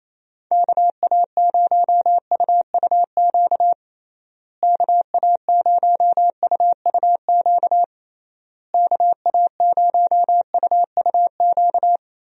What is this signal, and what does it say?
Morse code, heard as a keyed tone, about 28 wpm, 700 Hz. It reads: KA0UUQ KA0UUQ KA0UUQ
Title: USA Call Signs - 6 Characters - Repeated 3x - 28wpm